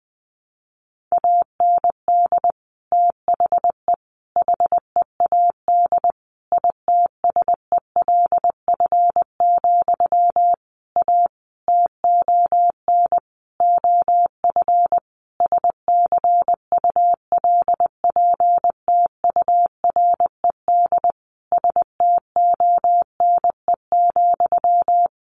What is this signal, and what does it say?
Morse code, heard as a keyed tone, about 20 wpm, 700 Hz. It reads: AND THE HEAD ITSELF, A TON OF SCULPTURED STONE,
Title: A Sound Of Thunder 20wpm